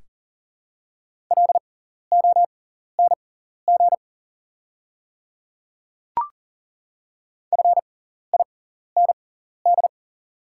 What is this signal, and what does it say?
Morse code, heard as a keyed tone, about 42 wpm, spaced as for 14 wpm, 690 Hz. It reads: LONG FIND
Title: 6x Character Spacing - Top 100 Words - 40wpm